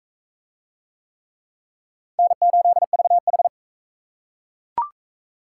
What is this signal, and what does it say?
Morse code, heard as a keyed tone, about 38 wpm, 700 Hz. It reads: N8VH